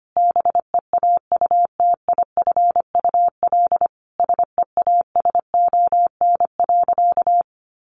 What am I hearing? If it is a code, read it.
Morse code, heard as a keyed tone, about 25 wpm, 700 Hz. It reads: BEAUTIFUL SEASON.
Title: Sentences from Top 1400 Words 25wpm